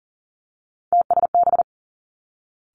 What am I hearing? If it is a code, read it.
Morse code, heard as a keyed tone, about 39 wpm, 700 Hz. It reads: TSB
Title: Three Challenging Letter-Number Combinations 40wpm